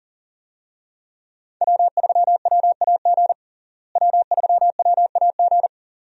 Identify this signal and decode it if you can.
Morse code, heard as a keyed tone, about 40 wpm, 700 Hz. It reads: W3WAG W3WAG
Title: USA Call Signs - 5 Characters - Repeated 2x - 40wpm